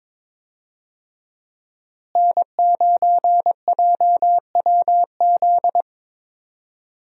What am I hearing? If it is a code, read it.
Morse code, heard as a keyed tone, about 22 wpm, 700 Hz. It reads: N9JWZ
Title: USA Call Signs - 5 Characters - Repeated 2x - 22wpm